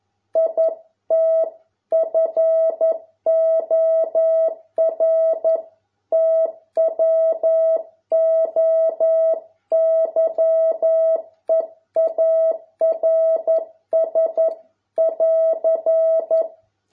Morse code, heard as a keyed tone, about 11 wpm, 650 Hz. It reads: IT FOR TWO YEARS <AR>